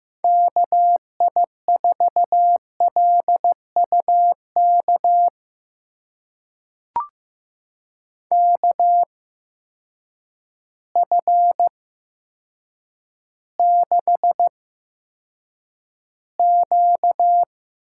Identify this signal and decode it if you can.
Morse code, heard as a keyed tone, about 15 wpm, spaced as for 8 wpm, 700 Hz. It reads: KI4LUK KF6Q
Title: USA Call Signs - 8x Character Spacing - 15wpm